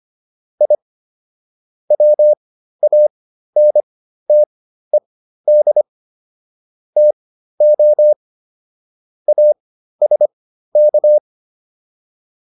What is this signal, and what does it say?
Morse code, heard as a keyed tone, about 25 wpm, spaced as for 13 wpm, 600 Hz. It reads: I WANTED TO ASK